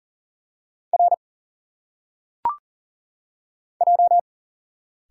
Morse code, heard as a keyed tone, about 39 wpm, 700 Hz. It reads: R J
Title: Single Letters - Phonetic Pronunciation 40wpm